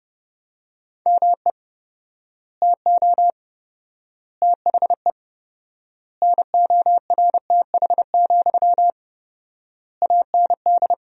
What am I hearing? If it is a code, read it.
Morse code, heard as a keyed tone, about 30 wpm, 700 Hz. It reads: ME TO THE NORTH, AND
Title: 4x Word-Spacing - Sentences from Top 400 Words - 30wpm